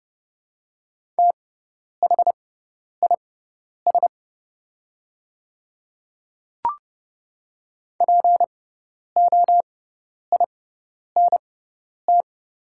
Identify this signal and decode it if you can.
Morse code, heard as a keyed tone, about 30 wpm, spaced as for 10 wpm, 690 Hz. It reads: THIS POINT
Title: 6x Character Spacing - Top 200 Words - 30wpm